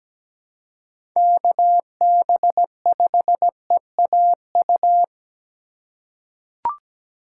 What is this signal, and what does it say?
Morse code, heard as a keyed tone, about 17 wpm, 700 Hz. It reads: KB5EAU
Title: USA Call Signs - 3x Character Spacing - 17wpm